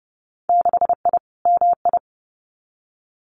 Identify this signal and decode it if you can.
Morse code, heard as a keyed tone, about 32 wpm, 700 Hz. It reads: 6I MI
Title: Winter Field Day Exchanges Encore 30wpm